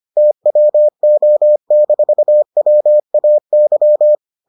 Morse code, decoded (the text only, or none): TWO-WAY